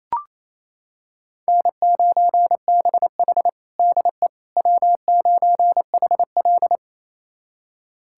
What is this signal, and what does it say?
Morse code, heard as a keyed tone, about 28 wpm, 700 Hz. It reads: N9BH DE W9HL
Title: USA Callsign DE Callsign - 4 Characters - 28wpm